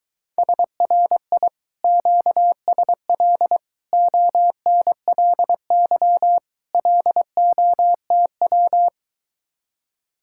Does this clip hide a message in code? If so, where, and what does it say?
Yes: Morse code from 0.4 to 8.9 s, heard as a keyed tone, about 22 wpm, 700 Hz. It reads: SRI QSL ONLY LOTW